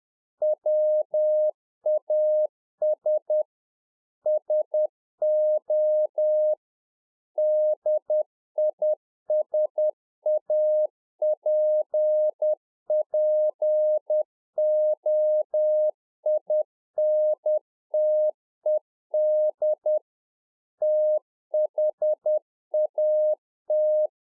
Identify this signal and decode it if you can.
Morse code, heard as a keyed tone, about 10 wpm, 610 Hz. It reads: WAS SO DISAPPOINTED THAT
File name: 2011-04-mp3 / A SMALL TRANCEIVER 1 10WPM